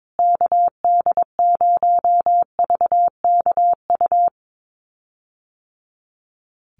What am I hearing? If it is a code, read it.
Morse code, heard as a keyed tone, about 22 wpm, 700 Hz. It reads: KD0VKU
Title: USA Call Signs Repeated 2x 22wpm